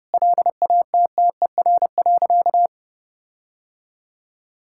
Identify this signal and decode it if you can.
Morse code, heard as a keyed tone, about 30 wpm, 700 Hz. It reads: LATTER.
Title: Sentences from Top 1600 Words 30wpm